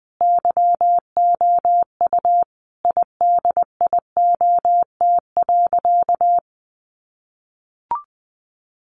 Morse code, heard as a keyed tone, about 20 wpm, 700 Hz. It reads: YOU IDIOT.
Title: A Sound Of Thunder 20wpm